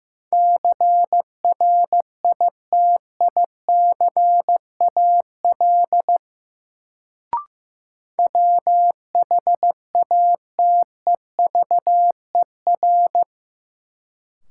Morse code, heard as a keyed tone, about 15 wpm, 700 Hz. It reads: CRITICAL WHATEVER
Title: Sentences from Top 1100 Words - New Word Drill - 15wpm